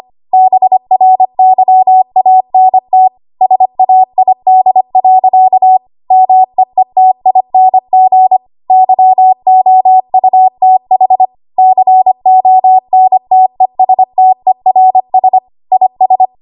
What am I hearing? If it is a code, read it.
Morse code, heard as a keyed tone, about 25 wpm, 750 Hz. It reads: BRYANT SAID. MEETING YOUTH CONTESTERS IS
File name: Archive / 25 WPM / 25WPM